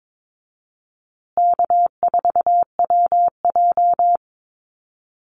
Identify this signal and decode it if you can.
Morse code, heard as a keyed tone, about 22 wpm, 700 Hz. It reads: K4WJ
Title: Easy Call Signs Repeated 3x - 22wpm